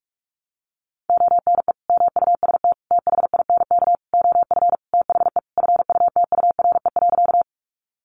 Morse code, heard as a keyed tone, about 44 wpm, 700 Hz. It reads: ONE MUST THINK OF THE FUTURE.
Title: Sentences from Top 700 Words 45wpm